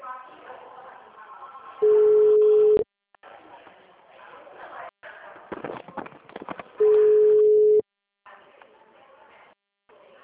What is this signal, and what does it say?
Ringback tone (Europe, 425 Hz), with speech.